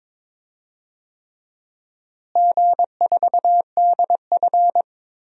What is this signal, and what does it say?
Morse code, heard as a keyed tone, about 22 wpm, 700 Hz. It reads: G4DF
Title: DXCC-100 Call Signs - 4 Characters - Repeated 3x - 22wpm